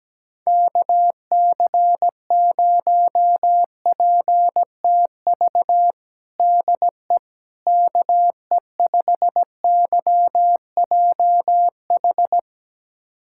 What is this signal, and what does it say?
Morse code, heard as a keyed tone, about 17 wpm, 700 Hz. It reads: KC0PTV DE KE5YJH
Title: USA Callsign DE Callsign 17wpm